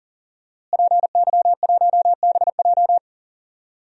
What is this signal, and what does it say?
Morse code, heard as a keyed tone, about 40 wpm, 700 Hz. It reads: PY1BJ
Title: DXCC-100 Call Signs - 5 Characters - Repeated 3x - 40wpm